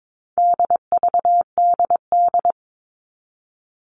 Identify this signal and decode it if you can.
Morse code, heard as a keyed tone, about 22 wpm, 700 Hz. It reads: DVDD
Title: All Cut Numbers - 4-Digit 22wpm